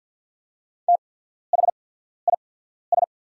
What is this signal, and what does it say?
Morse code, heard as a keyed tone, about 48 wpm, spaced as for 14 wpm, 700 Hz. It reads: THIS
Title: 8x Character Spacing - Top 300 Words - 50wpm